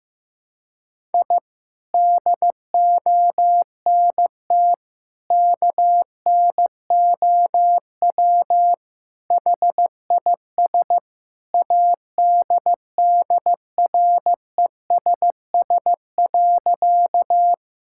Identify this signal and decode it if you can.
Morse code, heard as a keyed tone, about 15 wpm, 700 Hz. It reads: I DONT KNOW HIS ADDRESS.
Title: Sentences from Top 1000 Words - Review Part 6 - 15wpm